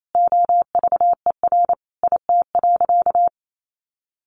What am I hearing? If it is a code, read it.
Morse code, heard as a keyed tone, about 28 wpm, 700 Hz. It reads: OVER IT.
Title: Sentences from Top 1500 Words 28wpm